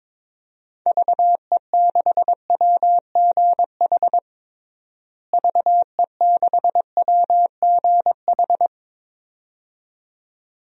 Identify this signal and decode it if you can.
Morse code, heard as a keyed tone, about 22 wpm, 700 Hz. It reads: VE6WGH VE6WGH
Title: DXCC-100 Call Signs - 6 Characters - Repeated 2x - 22wpm